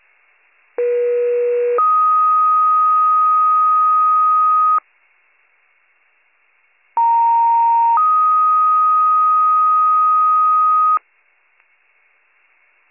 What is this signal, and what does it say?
Two-tone page, heard as a one-second tone, then a three-second tone, about 480 then 1250 Hz, about 930 then 1250 Hz. It is heard from 0.8 to 11.0 s.